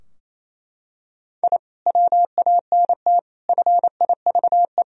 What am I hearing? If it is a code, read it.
Morse code, heard as a keyed tone, about 28 wpm, 700 Hz. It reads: I WANT FIVE
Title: Sentences from Top 1000 Words - Review Part 2 - 28wpm